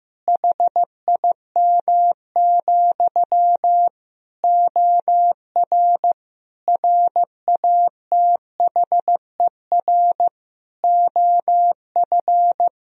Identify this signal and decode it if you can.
Morse code, heard as a keyed tone, about 15 wpm, 700 Hz. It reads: HIM, OR RATHER OF
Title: Sentences from Top 1000 Words - Review Part 4 - 15wpm